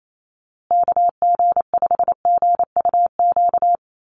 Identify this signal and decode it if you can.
Morse code, heard as a keyed tone, about 28 wpm, 690 Hz. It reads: KG5GUQ
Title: USA Call Signs Repeated 3x 28wpm